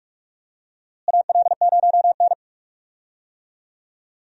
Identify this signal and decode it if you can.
Morse code, heard as a keyed tone, about 44 wpm, 700 Hz. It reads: AL0N
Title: USA Call Signs - 4 Characters - Repeated 3x - 45wpm